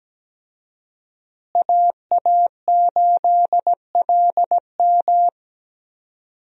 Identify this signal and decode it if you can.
Morse code, heard as a keyed tone, about 17 wpm, 700 Hz. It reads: AA8LM
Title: USA Call Signs - 5 Characters - Repeated 2x - 17wpm